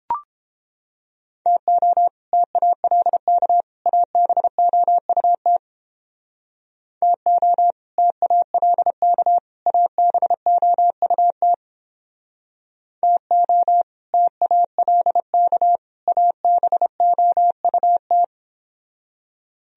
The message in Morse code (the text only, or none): TO TALK ABOUT TO TALK ABOUT TO TALK ABOUT